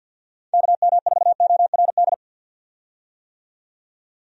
Morse code, heard as a keyed tone, about 52 wpm, 700 Hz. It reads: KM4ORD
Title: USA Call Signs - 6 Characters - 50wpm